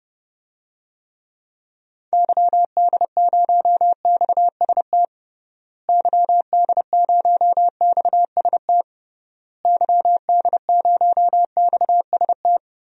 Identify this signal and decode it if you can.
Morse code, heard as a keyed tone, about 30 wpm, 700 Hz. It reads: YD0XST YD0XST YD0XST
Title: DXCC-100 Call Signs - 6 Characters - Repeated 3x - 30wpm